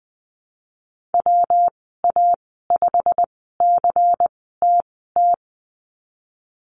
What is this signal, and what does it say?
Morse code, heard as a keyed tone, about 20 wpm, spaced as for 15 wpm, 700 Hz. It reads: WA5CTT
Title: 147 Lesson 24g - Letter y - Call Signs 20wpm